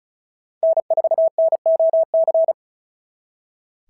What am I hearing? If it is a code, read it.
Morse code, heard as a keyed tone, about 36 wpm, 660 Hz. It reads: N4NOC
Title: SOTA QSO Practice 35wpm